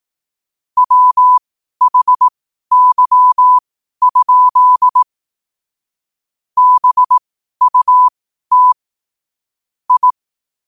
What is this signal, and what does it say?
Morse code, heard as a keyed tone, about 18 wpm, spaced as for 12 wpm, 1000 Hz. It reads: WHY? BUT I
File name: qotdfiles / 12wpm